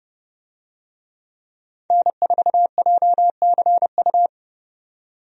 Morse code, heard as a keyed tone, about 30 wpm, 700 Hz. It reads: N4JCU
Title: USA Call Signs - 5 Characters - 30wpm